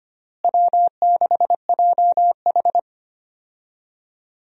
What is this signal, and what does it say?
Morse code, heard as a keyed tone, about 25 wpm, 700 Hz. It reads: W6JH
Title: USA Call Signs - 4 Characters - Repeated 3x - 25wpm